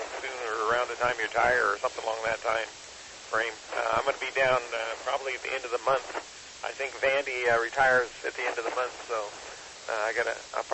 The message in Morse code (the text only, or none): none